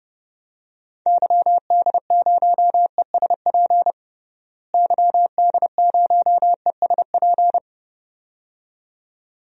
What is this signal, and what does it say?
Morse code, heard as a keyed tone, about 30 wpm, 700 Hz. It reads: YD0ESP YD0ESP
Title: DXCC-100 Call Signs - 6 Characters - Repeated 2x - 30wpm